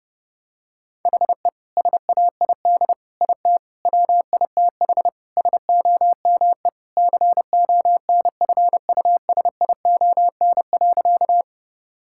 Morse code, heard as a keyed tone, about 30 wpm, 700 Hz. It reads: HE SAID IT WITH SOME CONFUSION.
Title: Sentences from Top 1600 Words 30wpm